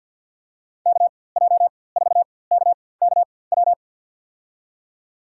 Morse code, heard as a keyed tone, about 52 wpm, spaced as for 24 wpm, 700 Hz. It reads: KJ4KKW
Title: USA Call Signs - 6 Characters - 4x Character Spacing - 50wpm